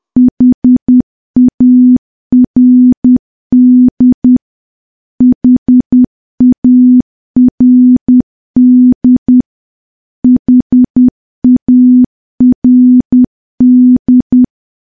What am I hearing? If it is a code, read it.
Morse code, heard as a keyed tone, about 10 wpm, 250 Hz. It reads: HARD HARD HARD
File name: morse / morse-training-10WPM